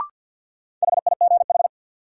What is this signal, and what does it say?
Morse code, heard as a keyed tone, about 50 wpm, 700 Hz. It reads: HIGH
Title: Sets of 2 Words - New Word Drill - 50wpm